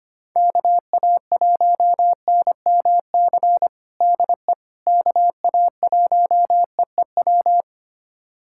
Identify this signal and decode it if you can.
Morse code, heard as a keyed tone, about 25 wpm, 700 Hz. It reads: KA1NMC DE KA1EEW
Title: USA Callsign DE Callsign 25wpm